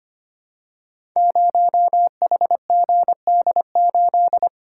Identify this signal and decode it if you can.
Morse code, heard as a keyed tone, about 25 wpm, 700 Hz. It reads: 0HGD8
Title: Code Groups - 5 Letter:Numbers - 25wpm